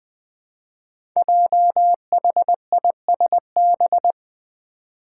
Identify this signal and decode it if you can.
Morse code, heard as a keyed tone, about 20 wpm, 700 Hz. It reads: JHISB
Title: Five Challenging Letter-Number Combinations 20wpm